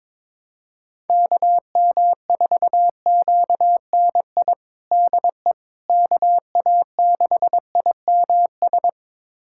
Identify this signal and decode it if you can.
Morse code, heard as a keyed tone, about 22 wpm, 700 Hz. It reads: KM4QNI DE KA6IMS
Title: USA Callsign DE Callsign - 6 Characters - 22wpm